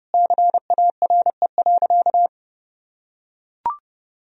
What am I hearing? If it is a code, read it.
Morse code, heard as a keyed tone, about 30 wpm, 700 Hz. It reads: CARE.
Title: Sentences from Top 1000 Words - Review Part 4 - 30wpm